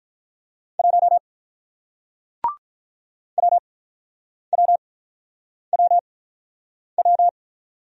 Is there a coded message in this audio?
Yes: Morse code from 0.8 to 7.3 s, heard as a keyed tone, about 48 wpm, 700 Hz. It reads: 1 W W W W